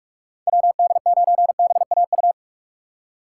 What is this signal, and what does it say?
Morse code, heard as a keyed tone, about 46 wpm, 700 Hz. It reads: WD9BAU
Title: USA Call Signs - 6 Characters - 6x Character Spacing - 45wpm